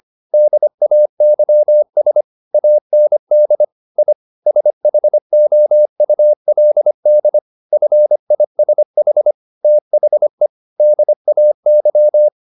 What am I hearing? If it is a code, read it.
Morse code, heard as a keyed tone, about 25 wpm, 600 Hz. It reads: DAYS AND I SHOULD FISH THE DAY
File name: the old man and the sea part 1 25wpm